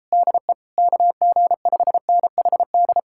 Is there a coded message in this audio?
Yes: Morse code from 0.1 to 3.0 s, heard as a keyed tone, about 37 wpm, 700 Hz. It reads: DE KG5NHD